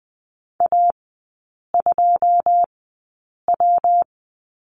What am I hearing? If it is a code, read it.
Morse code, heard as a keyed tone, about 20 wpm, spaced as for 8 wpm, 700 Hz. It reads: A2W
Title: 114 Lesson 19e - Letter w - Three Characters 20wpm